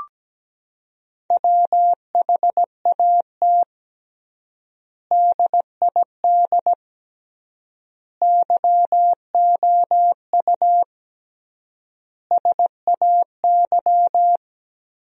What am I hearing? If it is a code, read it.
Morse code, heard as a keyed tone, about 17 wpm, 700 Hz. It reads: WHAT DID YOU SAY
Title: 3x Word-Spacing - Sentences from Top 200 Words - 17wpm